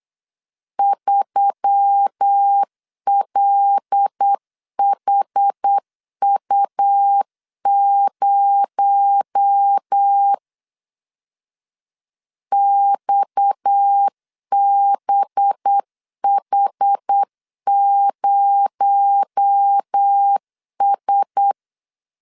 Morse code, heard as a keyed tone, about 8 wpm, 800 Hz. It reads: 3LHU0 XBH0S
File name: LEZIONI / 7 GF9Z / 7 GF9 8wpm 3545